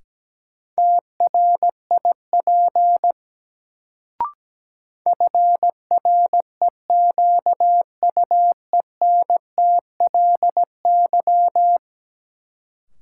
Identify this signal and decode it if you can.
Morse code, heard as a keyed tone, about 17 wpm, 700 Hz. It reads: TRIP FREQUENTLY
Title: Sentences from Top 1300 Words - New Word Drill Alt - 17wpm